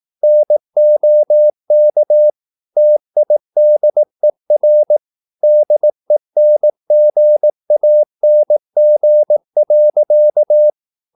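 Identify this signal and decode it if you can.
Morse code, heard as a keyed tone, about 18 wpm, 600 Hz. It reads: NOK TIDER DENGANG.
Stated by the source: Slow (QRS) Morse practice at 18 wpm, CW med Gnister.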